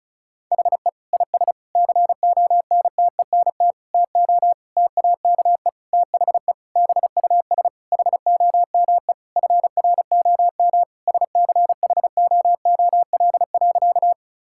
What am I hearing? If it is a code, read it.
Morse code, heard as a keyed tone, about 34 wpm, 700 Hz. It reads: HE IS CONTENT TO TAKE THE BUS HOME FROM SCHOOL.
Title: Sentences from Top 1000 Words - Review Part 6 - 35wpm